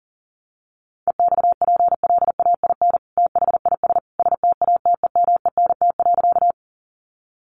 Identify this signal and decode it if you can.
Morse code, heard as a keyed tone, about 40 wpm, 700 Hz. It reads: EXPLAIN THIS STATEMENT.
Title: Sentences from Top 1200 Words 40wpm